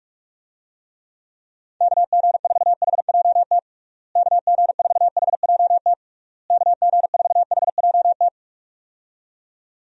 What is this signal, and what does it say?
Morse code, heard as a keyed tone, about 46 wpm, 700 Hz. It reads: KG4HJT KG4HJT KG4HJT
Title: USA Call Signs - 6 Characters - Repeated 3x - 45wpm